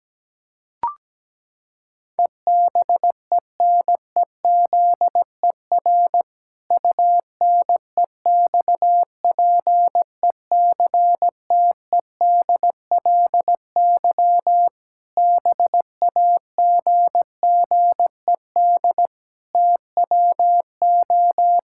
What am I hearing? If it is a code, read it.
Morse code, heard as a keyed tone, about 17 wpm, 700 Hz. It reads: EBENEZER UNEXPECTEDLY BAGGED TWO
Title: Pangram Challenge v2 - 17wpm